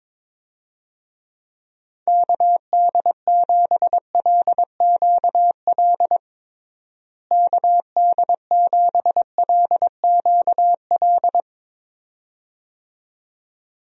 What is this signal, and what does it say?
Morse code, heard as a keyed tone, about 21 wpm, 700 Hz. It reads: KD7LQL KD7LQL
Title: USA Call Signs - 6 Characters - Repeated 2x - 22wpm